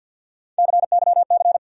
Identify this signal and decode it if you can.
Morse code, heard as a keyed tone, about 51 wpm, 700 Hz. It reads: CYC
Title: USA Call Sign Postfixes - 3 Letters 50wpm